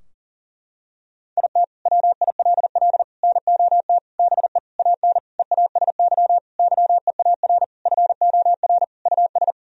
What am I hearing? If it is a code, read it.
Morse code, heard as a keyed tone, about 40 wpm, 700 Hz. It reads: IT WILL NOT BE AN EASY YEAR FOR US
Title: Sentences from Top 1000 Words - Review Part 2 - 40wpm